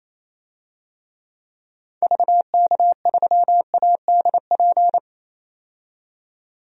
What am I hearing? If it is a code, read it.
Morse code, heard as a keyed tone, about 28 wpm, 700 Hz. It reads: VK3ADP
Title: DXCC-100 Call Signs - 6 Characters - 28wpm